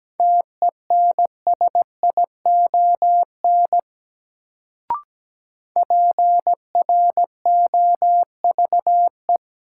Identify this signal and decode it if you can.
Morse code, heard as a keyed tone, about 17 wpm, 700 Hz. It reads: TENSION PROVE
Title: Sentences from Top 1600 Words - New Word Drill - 17wpm